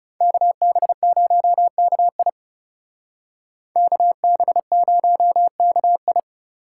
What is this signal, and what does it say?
Morse code, heard as a keyed tone, about 38 wpm, 700 Hz. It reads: KB0KI KB0KI